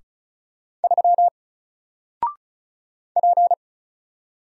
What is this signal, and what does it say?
Morse code, heard as a keyed tone, about 35 wpm, 700 Hz. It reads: 3 P
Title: Single Character (Letter:Number:Punctuation:Prosign) 35wpm